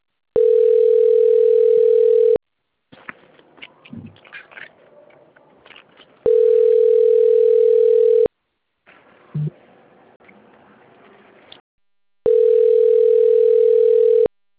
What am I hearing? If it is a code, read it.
Ringback tone (North America), 440 + 480 Hz.